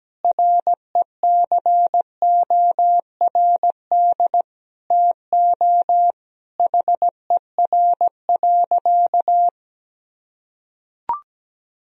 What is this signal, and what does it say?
Morse code, heard as a keyed tone, about 17 wpm, 700 Hz. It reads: RECORD TO HER.
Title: Sentences from Top 1000 Words - Review Part 3 - 17wpm